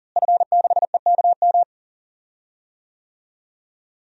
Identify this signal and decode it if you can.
Morse code, heard as a keyed tone, about 40 wpm, 700 Hz. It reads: F6EKM
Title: DXCC-100 Call Signs - 5 Characters - Repeated 2x - 40wpm